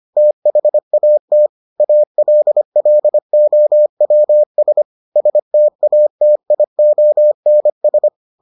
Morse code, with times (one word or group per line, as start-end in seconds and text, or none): 0.2-1.5 THAT
1.8-4.8 ALLOWS
5.2-8.1 STATIONS